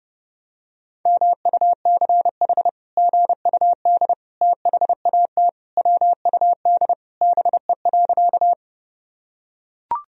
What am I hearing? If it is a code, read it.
Morse code, heard as a keyed tone, about 30 wpm, 700 Hz. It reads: MUCH GUD THAT WUD BE.
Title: Sentences from Top 200 Words and Abbreviations 30wpm